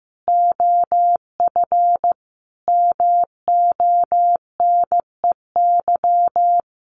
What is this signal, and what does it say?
Morse code, heard as a keyed tone, about 15 wpm, 700 Hz. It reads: OF MONEY
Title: Sets of 5 Words Encore2 15wpm